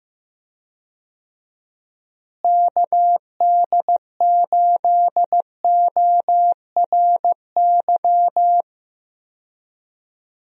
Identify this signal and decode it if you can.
Morse code, heard as a keyed tone, about 15 wpm, 700 Hz. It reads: KD8ORY
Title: USA Call Signs - 6 Characters - Repeated 2x - 15wpm